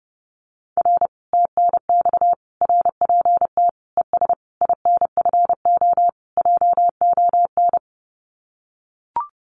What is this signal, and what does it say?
Morse code, heard as a keyed tone, about 30 wpm, 700 Hz. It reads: R TNX RPT ES INFO JON
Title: Common QSO Phrases Encore 30wpm